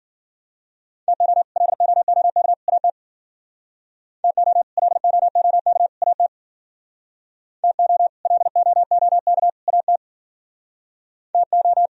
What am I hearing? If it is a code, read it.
Morse code, heard as a keyed tone, about 52 wpm, 700 Hz. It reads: TO LOOK AT TO LOOK AT TO LOOK AT TO